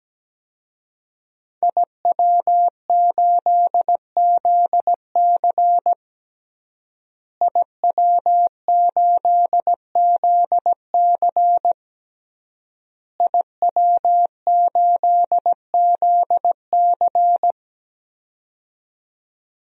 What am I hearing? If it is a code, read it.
Morse code, heard as a keyed tone, about 17 wpm, 700 Hz. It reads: IW8ZC IW8ZC IW8ZC
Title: DXCC-100 Call Signs - 5 Characters - Repeated 3x - 17wpm